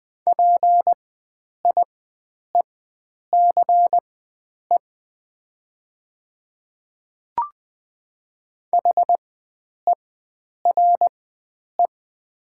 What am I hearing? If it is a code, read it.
Morse code, heard as a keyed tone, about 20 wpm, spaced as for 9 wpm, 700 Hz. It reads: PIECE HERE
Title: 4x Character Spacing - Top 500 Words - 20wpm